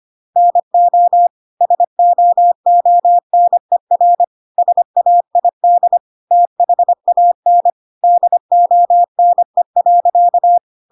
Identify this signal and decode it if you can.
Morse code, heard as a keyed tone, about 25 wpm, 700 Hz. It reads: NO SOONER SAID THAN DONE.
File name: Pinocchio / Pinocchio ch5 25 25 ik1hge